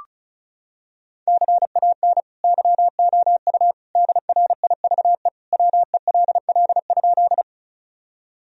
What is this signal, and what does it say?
Morse code, heard as a keyed tone, about 36 wpm, 700 Hz. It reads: CAN YOU DRIVE WELL?
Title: Sentences from Top 1000 Words - Review Part 2 - 35wpm